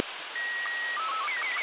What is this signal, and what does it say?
SSTV (slow-scan television), Robot 36.